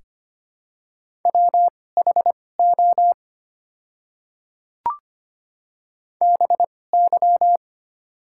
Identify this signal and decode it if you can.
Morse code, heard as a keyed tone, about 25 wpm, spaced as for 18 wpm, 700 Hz. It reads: WHO BY
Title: 2x Character Spacing - Top 100 Words - 25wpm